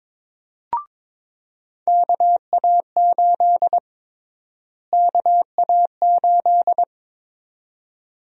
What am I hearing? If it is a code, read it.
Morse code, heard as a keyed tone, about 22 wpm, 700 Hz. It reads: KA8 KA8
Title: USA Call Sign Prefixes - Repeated 2x - 22wpm